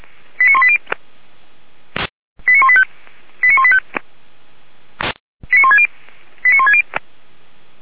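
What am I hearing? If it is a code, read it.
Five-tone selective call (ZVEI), heard as quick runs of five tones, tones of about 70 ms. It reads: 89180 89185 89185 89160 89160
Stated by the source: Five-tone ZVEI selective call received by SDR, from the audio server of Feuerwehr Orlamünde (fire brigade).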